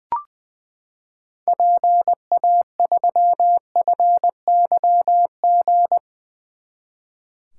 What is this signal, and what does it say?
Morse code, heard as a keyed tone, about 20 wpm, 700 Hz. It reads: PA3FYG
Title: 2023 SOTA Activator Call Signs - 20wpm